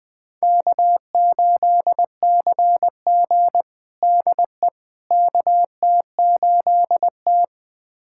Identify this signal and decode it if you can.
Morse code, heard as a keyed tone, about 20 wpm, 700 Hz. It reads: K8CG DE KT8T
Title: USA Callsign DE Callsign - 4 Characters - 20wpm